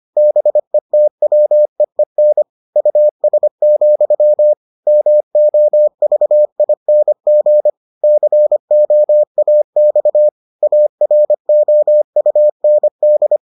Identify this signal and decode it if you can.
Morse code, heard as a keyed tone, about 25 wpm, 600 Hz. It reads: BETWEEN US, MOVING COAX AROUND